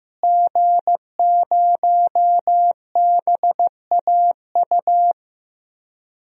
Morse code, heard as a keyed tone, about 15 wpm, 700 Hz. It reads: G0BAU
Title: DXCC-100 Call Signs - 5 Characters - Repeated 3x - 15wpm